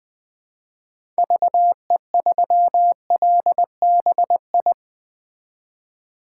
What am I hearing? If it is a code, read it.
Morse code, heard as a keyed tone, about 20 wpm, 700 Hz. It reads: VE3LBI
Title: 2023 POTA Hunter Call Signs - 20wpm